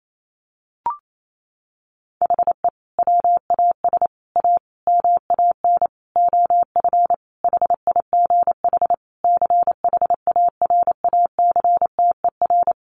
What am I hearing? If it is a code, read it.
Morse code, heard as a keyed tone, about 28 wpm, 700 Hz. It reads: HE WAS A MAN OF HIGH CHARACTER
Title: Sentences from Top 1200 Words 28wpm